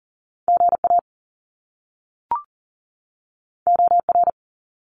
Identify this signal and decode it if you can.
Morse code, heard as a keyed tone, about 40 wpm, 700 Hz. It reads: GA OR
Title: Top Two Letter Combinations in Words 40wpm